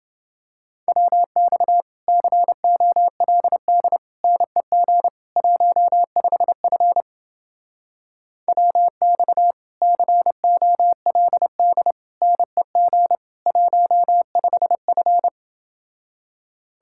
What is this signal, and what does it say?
Morse code, heard as a keyed tone, about 29 wpm, 700 Hz. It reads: WX COLD NEG 15F WX COLD NEG 15F